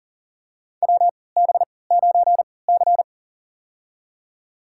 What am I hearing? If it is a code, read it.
Morse code, heard as a keyed tone, about 40 wpm, spaced as for 23 wpm, 700 Hz. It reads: WB9C
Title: USA Call Signs - 4 Characters - 3x Character Spacing - 40wpm